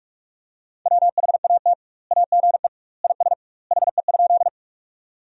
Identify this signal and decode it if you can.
Morse code, heard as a keyed tone, about 46 wpm, 700 Hz. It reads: WHAT AGE IS HE?
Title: 2x Word-Spacing - Sentences from Top 500 Words - 45wpm